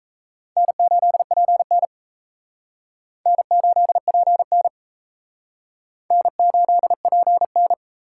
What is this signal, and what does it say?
Morse code, heard as a keyed tone, about 35 wpm, 700 Hz. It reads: N8PN N8PN N8PN
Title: Speed Racing - USA Call Signs 28wpm